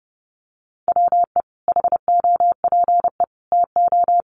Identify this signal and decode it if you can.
Morse code, heard as a keyed tone, about 30 wpm, 700 Hz. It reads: WE HOPE TO
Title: Sentences from Top 1100 Words 30wpm